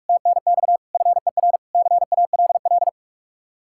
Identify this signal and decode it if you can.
Morse code, heard as a keyed tone, about 44 wpm, 700 Hz. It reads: TNX FER CALL